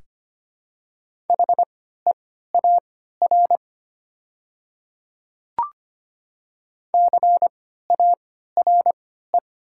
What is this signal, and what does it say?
Morse code, heard as a keyed tone, about 25 wpm, spaced as for 14 wpm, 700 Hz. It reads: HEAR CARE
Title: 3x Character Spacing - Top 300 Words - 25wpm